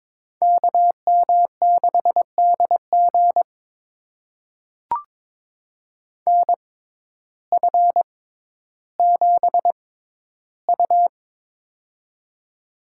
Morse code, heard as a keyed tone, about 22 wpm, spaced as for 12 wpm, 700 Hz. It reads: KM6DG NF7U
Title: USA Call Signs - 6x Character Spacing - 22wpm